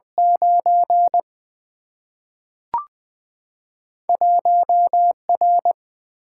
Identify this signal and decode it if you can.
Morse code, heard as a keyed tone, about 20 wpm, 700 Hz. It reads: 9 1R
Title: Single-Letter Single-Number 20wpm